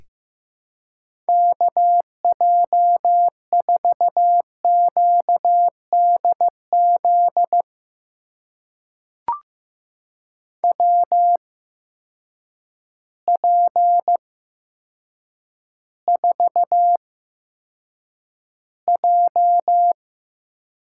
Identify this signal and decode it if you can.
Morse code, heard as a keyed tone, about 15 wpm, spaced as for 8 wpm, 700 Hz. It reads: KJ4QDZ WP4J
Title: USA Call Signs - 6 Characters - 8x Character Spacing - 15wpm